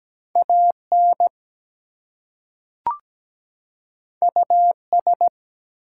Morse code, heard as a keyed tone, about 17 wpm, 700 Hz. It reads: AN US
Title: Two Letter Words - 17wpm